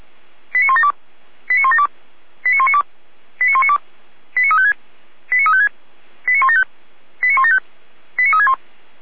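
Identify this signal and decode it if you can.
Five-tone selective call (ZVEI), heard as quick runs of five tones, tones of about 70 ms. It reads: 89172 89172 89192 89192 89356 89356 89175 89175 89361
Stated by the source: ZVEI five-tone selective call 89013 received on an SDR.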